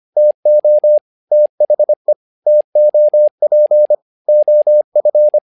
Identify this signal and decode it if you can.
Morse code, heard as a keyed tone, about 25 wpm, 600 Hz. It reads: TO THE TOP OF
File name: learning center / tower trouble / tower trouble 25wpm